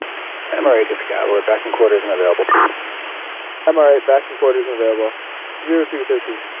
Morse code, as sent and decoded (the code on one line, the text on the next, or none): none